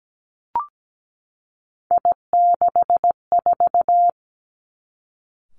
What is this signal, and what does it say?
Morse code, heard as a keyed tone, about 17 wpm, 700 Hz. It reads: I64
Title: Three Challenging Letter-Number Combinations 17wpm